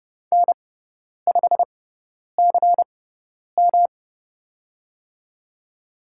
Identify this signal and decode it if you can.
Morse code, heard as a keyed tone, about 30 wpm, spaced as for 10 wpm, 700 Hz. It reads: N5CM
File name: Lesson 18g - Letter m - Call Signs 30wpm